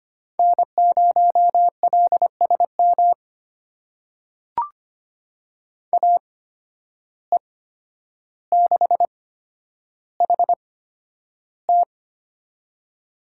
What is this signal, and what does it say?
Morse code, heard as a keyed tone, about 25 wpm, spaced as for 11 wpm, 700 Hz. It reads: N0LSM AE6HT
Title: USA Call Signs - 5 Characters - 8x Character Spacing - 25wpm